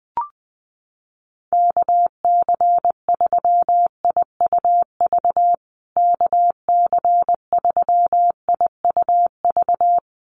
Morse code, heard as a keyed tone, about 20 wpm, 700 Hz. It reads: KC3IUV KC3IUV
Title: USA Call Signs Repeated 2x 20wpm